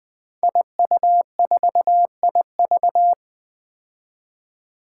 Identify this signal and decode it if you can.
Morse code, heard as a keyed tone, about 20 wpm, 700 Hz. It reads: IU4IV
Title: Five Challenging Letter-Number Combinations 20wpm